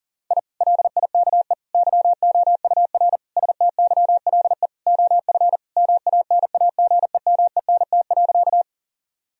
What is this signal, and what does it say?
Morse code, heard as a keyed tone, about 37 wpm, 700 Hz. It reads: I LIKE YOUR STYLE OF MANAGEMENT.
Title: Sentences from Top 1000 Words - Review Part 4 - 40wpm